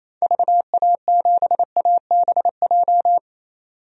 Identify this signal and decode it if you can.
Morse code, heard as a keyed tone, about 28 wpm, 700 Hz. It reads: VA7ABJ
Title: DXCC-100 Call Signs - 6 Characters - 28wpm